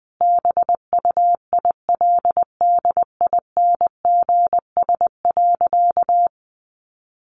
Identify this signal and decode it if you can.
Morse code, heard as a keyed tone, about 20 wpm, 700 Hz. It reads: BUILDINGS.
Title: Easy Phrases 20wpm